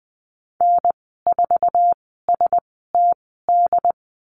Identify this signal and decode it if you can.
Morse code, heard as a keyed tone, about 20 wpm, spaced as for 15 wpm, 700 Hz. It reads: N4STD
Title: 082 Lesson 14g - Number 2 - Call Signs 20wpm